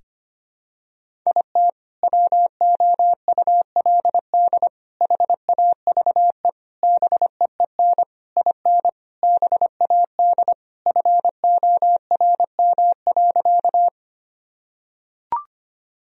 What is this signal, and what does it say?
Morse code, heard as a keyed tone, about 25 wpm, 700 Hz. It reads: IT WOULD HAVE BEEN IN BAD FORM.
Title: Sentences from Top 1000 Words - Review Part 1 - 25wpm